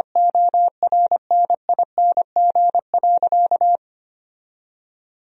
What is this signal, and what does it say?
Morse code, heard as a keyed tone, about 25 wpm, 700 Hz. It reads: ORNING.
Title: Sentences from Top 1000 Words - Review Part 2 - 25wpm